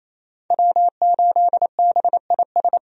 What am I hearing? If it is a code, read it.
Morse code, heard as a keyed tone, about 28 wpm, 700 Hz. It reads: W8BIS
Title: USA Call Signs - 5 Characters - Repeated 3x - 28wpm